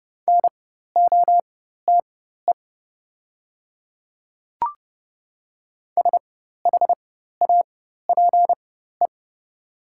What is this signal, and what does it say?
Morse code, heard as a keyed tone, about 30 wpm, spaced as for 14 wpm, 700 Hz. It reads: NOTE SHAPE
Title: 4x Character Spacing - Top 500 Words - 30wpm